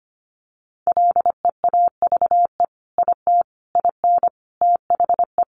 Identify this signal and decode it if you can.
Morse code, heard as a keyed tone, about 25 wpm, 700 Hz. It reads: LEAVE IT IN THE
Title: Sentences from Top 1300 Words 25wpm